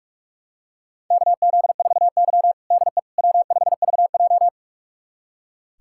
Morse code, heard as a keyed tone, about 46 wpm, 700 Hz. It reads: KZ4Y DE W5VJ
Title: USA Callsign DE Callsign - 4 Characters - 45wpm